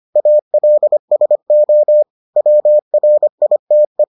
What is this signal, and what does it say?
Morse code, heard as a keyed tone, about 25 wpm, 600 Hz. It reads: ALSO WRITE